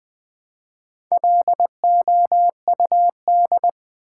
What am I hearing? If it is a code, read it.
Morse code, heard as a keyed tone, about 20 wpm, 700 Hz. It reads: LOUD
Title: Build A Word - 1 Letter At A Time - 3 to 5-Letter Words - 20wpm